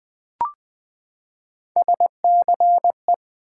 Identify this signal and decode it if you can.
Morse code, heard as a keyed tone, about 20 wpm, 700 Hz. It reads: SCE
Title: Build A Word - 3 Letters At A Time - 8-Letter Words - 20wpm